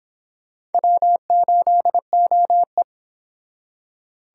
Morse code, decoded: W8OE

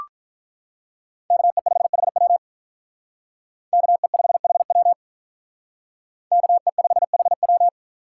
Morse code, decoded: KE5HW KE5HW KE5HW

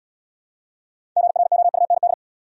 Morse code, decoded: GROUND

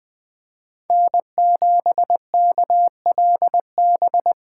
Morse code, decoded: N7KLB

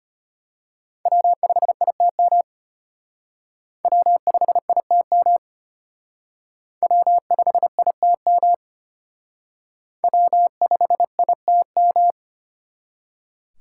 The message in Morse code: W5ITM W5ITM W5ITM W5ITM

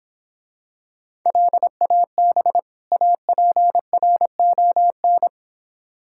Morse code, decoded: LAB APRON

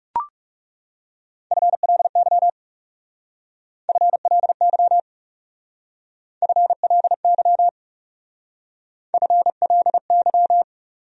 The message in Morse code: FLY FLY FLY FLY

